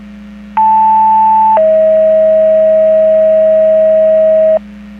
Two-tone page, heard as a one-second tone, then a three-second tone, about 880 then 640 Hz, with noise. It is heard from 0.6 to 4.6 s.